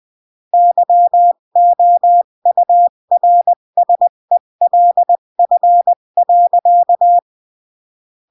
Morse code, heard as a keyed tone, about 20 wpm, 700 Hz. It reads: YOURSELF.
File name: CW / jodorowsky / 82 commandments 20-15